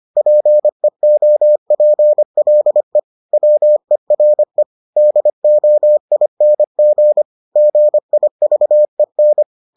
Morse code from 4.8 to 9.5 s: DOING GIVEN

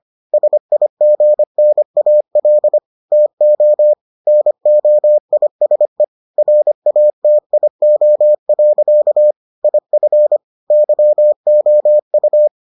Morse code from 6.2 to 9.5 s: RATIO.

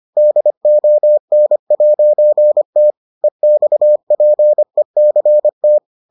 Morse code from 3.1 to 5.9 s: EXPECT

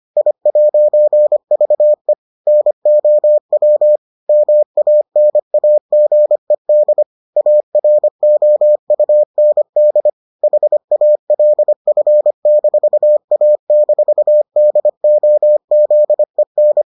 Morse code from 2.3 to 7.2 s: NOW MANAGED

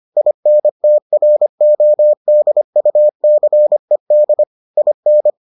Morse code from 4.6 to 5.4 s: IN

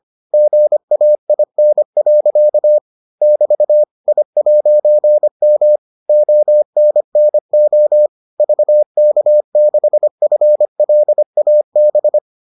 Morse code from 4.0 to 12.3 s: I'M ONNO VK6FLAB

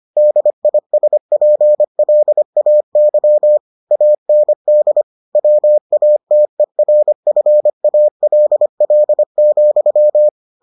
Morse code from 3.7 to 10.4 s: AND WATERFALL,

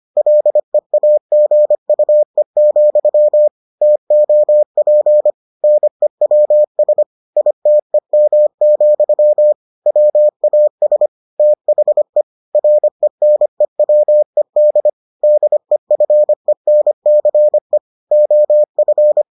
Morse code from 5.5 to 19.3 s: NEWS ITEM, WAS THE RENEWED DEFENCE OF

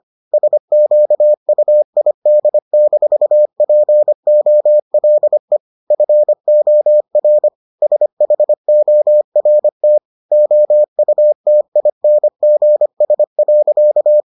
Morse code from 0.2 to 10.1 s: SQUID-POLE FOR SHORT

